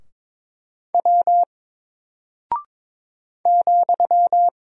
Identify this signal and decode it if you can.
Morse code, heard as a keyed tone, about 22 wpm, 700 Hz. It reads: W ,
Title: Single Character (Letter:Number:Punctuation:Prosign) Alt 22wpm